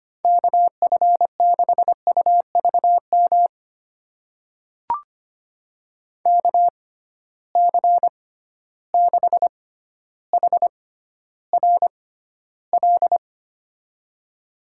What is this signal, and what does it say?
Morse code, heard as a keyed tone, about 25 wpm, spaced as for 13 wpm, 700 Hz. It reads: KF6UVM KC6HRL
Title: USA Call Signs - 6 Characters - 6x Character Spacing - 25wpm